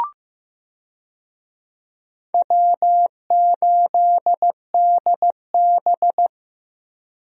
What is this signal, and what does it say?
Morse code, heard as a keyed tone, about 15 wpm, 700 Hz. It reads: W8DB
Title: USA Call Signs - 4 Characters - 15wpm